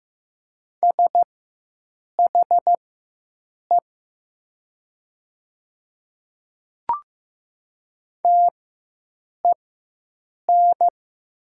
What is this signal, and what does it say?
Morse code, heard as a keyed tone, about 15 wpm, spaced as for 7 wpm, 690 Hz. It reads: SHE TEN
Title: 4x Character Spacing - Top 500 Words - 15wpm